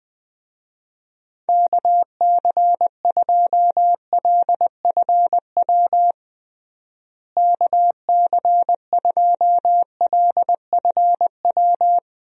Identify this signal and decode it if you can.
Morse code, heard as a keyed tone, about 20 wpm, 700 Hz. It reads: KC2LFW KC2LFW
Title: USA Call Signs - 6 Characters - Repeated 2x - 20wpm